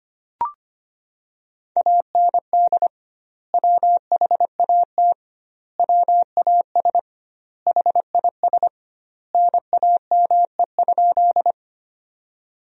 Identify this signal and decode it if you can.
Morse code, heard as a keyed tone, about 25 wpm, 700 Hz. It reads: AND WHAT WAS HIS NAME?
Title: 2x Word-Spacing - Sentences from Top 200 Words - 25wpm